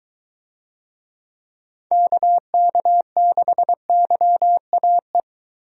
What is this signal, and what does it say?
Morse code, heard as a keyed tone, about 21 wpm, 700 Hz. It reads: KK6YAE